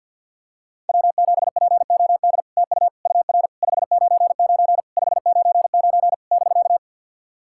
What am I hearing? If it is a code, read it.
Morse code, heard as a keyed tone, about 50 wpm, 700 Hz. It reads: W7POD TU UR 599 599 <BK>
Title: SOTA QSO Practice 50wpm